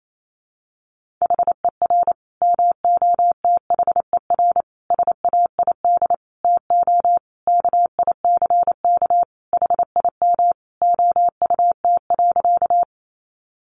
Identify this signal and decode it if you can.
Morse code, heard as a keyed tone, about 28 wpm, 700 Hz. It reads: HER MOTHER SAID TO KICK HIM OUT.
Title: Sentences from Top 1300 Words 28wpm